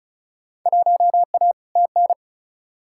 Morse code, heard as a keyed tone, about 36 wpm, 700 Hz. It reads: 1A TN
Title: Field Day Exchanges - 35wpm